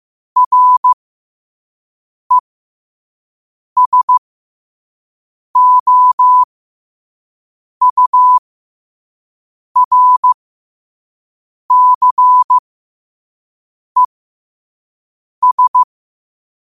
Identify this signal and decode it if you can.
Morse code, heard as a keyed tone, about 15 wpm, spaced as for 5 wpm, 1000 Hz. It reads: RESOURCES
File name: qotdfiles / 5wpm